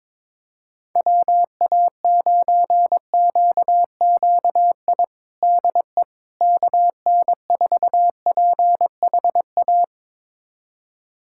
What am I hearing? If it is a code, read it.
Morse code, heard as a keyed tone, about 22 wpm, 700 Hz. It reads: WA9QQI DE KN4PHA
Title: USA Callsign DE Callsign 22wpm